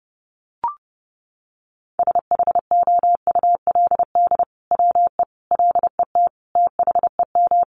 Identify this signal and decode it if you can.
Morse code, heard as a keyed tone, about 30 wpm, 700 Hz. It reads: SHOULD WE LET THEM
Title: Sentences from Top 600 Words 30wpm